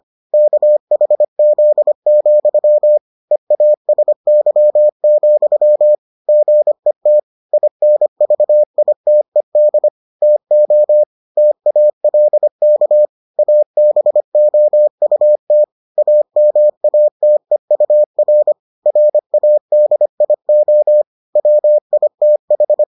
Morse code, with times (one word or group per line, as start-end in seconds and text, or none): 0.3-3.0 KHZ,
3.3-5.9 EASY,
6.3-7.2 GET
7.5-9.9 INVITED
10.2-11.0 TO
11.4-13.1 TALK
13.4-15.6 ABOUT
16.0-18.5 AMATEUR
18.9-21.0 RADIO
21.4-22.8 WITH